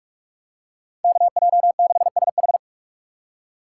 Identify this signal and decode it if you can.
Morse code, heard as a keyed tone, about 42 wpm, 700 Hz. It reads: KJ6SH